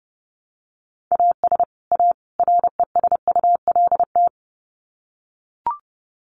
Morse code, heard as a keyed tone, about 30 wpm, 700 Hz. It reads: AS A RESULT
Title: Sets of 3 Words 30wpm